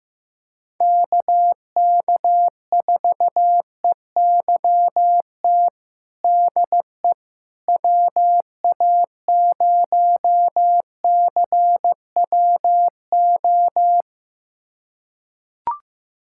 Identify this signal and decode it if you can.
Morse code, heard as a keyed tone, about 15 wpm, 700 Hz. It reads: KK4EYT DE WA0CWO
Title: USA Callsign DE Callsign - 6 Characters - 15wpm